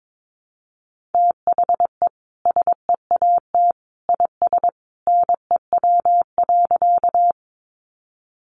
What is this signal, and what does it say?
Morse code, heard as a keyed tone, about 22 wpm, 700 Hz. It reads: THE SEAT IS NEW.
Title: Sentences from Top 1500 Words 22wpm